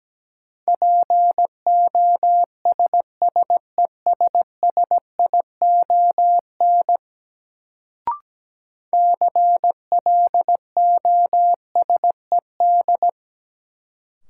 Morse code, heard as a keyed tone, about 17 wpm, 700 Hz. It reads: POSSESSION CLOSED
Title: Sentences from Top 1400 Words - New Word Drill - 17wpm